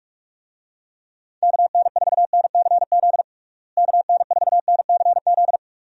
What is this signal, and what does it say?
Morse code, heard as a keyed tone, about 46 wpm, 700 Hz. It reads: KN4NCZ KN4NCZ
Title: USA Call Signs - 6 Characters - Repeated 3x - 45wpm